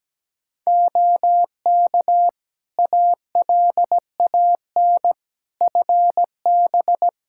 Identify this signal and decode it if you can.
Morse code, heard as a keyed tone, about 17 wpm, 700 Hz. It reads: OK ALAN FB
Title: Practice QSO - Exchange 2 - Info - 17wpm